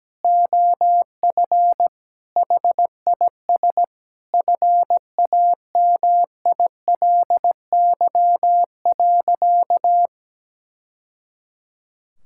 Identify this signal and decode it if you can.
Morse code, heard as a keyed tone, about 17 wpm, 700 Hz. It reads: OF HIS FAMILY.
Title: Sentences from Top 1000 Words - Review Part 6 - 17wpm